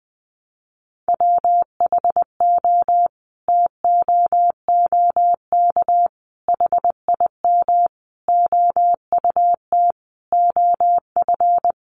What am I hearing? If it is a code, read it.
Morse code, heard as a keyed tone, about 20 wpm, 700 Hz. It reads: WHO TOOK HIM OUT OF
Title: Sentences from Top 300 Words 20wpm